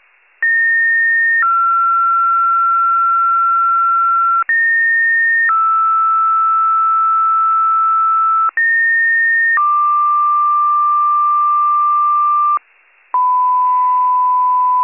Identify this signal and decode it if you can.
Two-tone page, heard as a one-second tone, then a three-second tone, about 1850 then 1360 Hz, about 1850 then 1320 Hz, about 1850 then 1180 Hz, with noise. It is heard from 0.4 to 12.6 s.